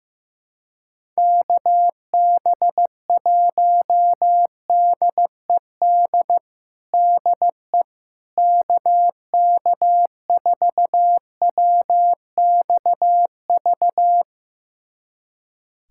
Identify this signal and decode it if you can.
Morse code, heard as a keyed tone, about 15 wpm, 700 Hz. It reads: KB1DED DE KK4WXV
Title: USA Callsign DE Callsign - 6 Characters - 15wpm